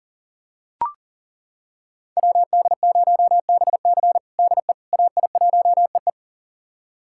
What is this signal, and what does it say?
Morse code, heard as a keyed tone, about 40 wpm, 700 Hz. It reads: WD0BC DE AI1EE
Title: USA Callsign DE Callsign - 5 Characters - 40wpm